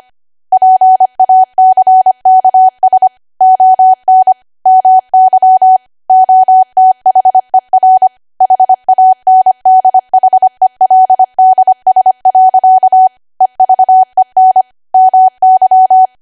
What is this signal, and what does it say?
Morse code, heard as a keyed tone, about 25 wpm, 750 Hz. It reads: PACKS ON MY OTHER HANDHELDS. EVEN MY
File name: Archive / 25 WPM / 25WPM